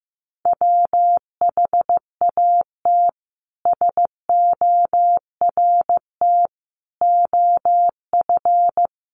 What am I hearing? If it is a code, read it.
Morse code, heard as a keyed tone, about 15 wpm, 700 Hz. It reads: WHAT SORT OF
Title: Sentences from Top 1300 Words 15wpm